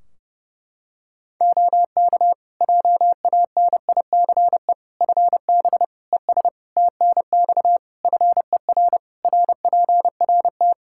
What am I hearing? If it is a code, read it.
Morse code, heard as a keyed tone, about 30 wpm, 700 Hz. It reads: OK JANICE FB ES TNX FER RPRT
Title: K1USN SST QSO Practice 30wpm